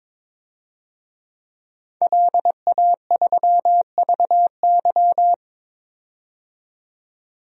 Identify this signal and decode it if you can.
Morse code, heard as a keyed tone, about 21 wpm, 700 Hz. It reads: LA3VY
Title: DXCC-100 Call Signs - 5 Characters - 22wpm